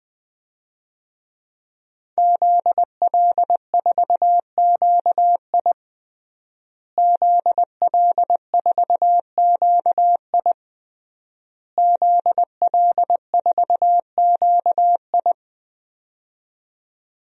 Morse code, heard as a keyed tone, about 20 wpm, 700 Hz. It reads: ZL4QI ZL4QI ZL4QI
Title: DXCC-100 Call Signs - 5 Characters - Repeated 3x - 20wpm